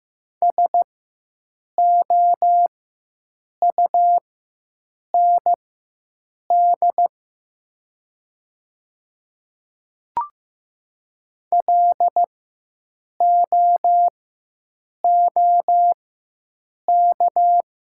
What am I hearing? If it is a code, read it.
Morse code, heard as a keyed tone, about 15 wpm, spaced as for 7 wpm, 700 Hz. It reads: SOUND LOOK
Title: 4x Character Spacing - Top 100 Words - 15wpm